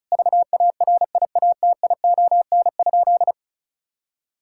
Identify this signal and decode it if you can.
Morse code, heard as a keyed tone, about 36 wpm, 700 Hz. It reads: VARIATION?
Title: Sentences from Top 1600 Words 35wpm